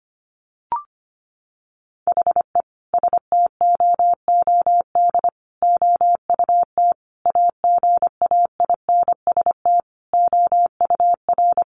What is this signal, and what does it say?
Morse code, heard as a keyed tone, about 25 wpm, 700 Hz. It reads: HE STOOD OUT AGAINST OUR
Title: Sentences from Top 1500 Words 25wpm